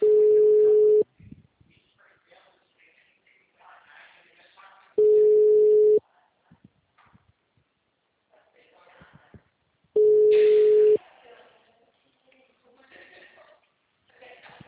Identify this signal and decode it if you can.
Ringback tone (Europe, 425 Hz), with speech.
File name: out- -2011- - - .412